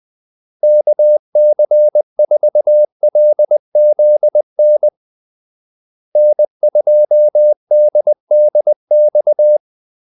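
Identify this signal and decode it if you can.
Morse code, heard as a keyed tone, about 20 wpm, 600 Hz. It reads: KC4LZN N2DDX
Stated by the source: Morse call-sign practice at 20 wpm.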